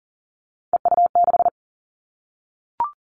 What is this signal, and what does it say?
Morse code, heard as a keyed tone, about 40 wpm, 700 Hz. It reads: EU6
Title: Callsign Prefixes for Top 50 DXCC - 40wpm